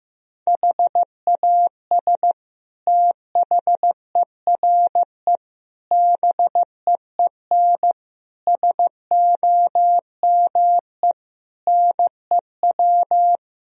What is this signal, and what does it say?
Morse code, heard as a keyed tone, about 15 wpm, 700 Hz. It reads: HAS THERE BEEN SOME NEW
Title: Sentences from Top 1000 Words - Review Part 4 - 15wpm